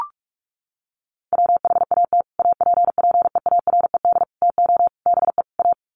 Morse code, heard as a keyed tone, about 44 wpm, 700 Hz. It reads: WHAT APPEARED TO BE A
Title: Sets of 5 Words Encore2 45wpm